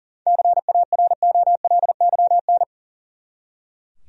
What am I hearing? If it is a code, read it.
Morse code, heard as a keyed tone, about 40 wpm, 700 Hz. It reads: CAROLYN
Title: USA Common Names - 40wpm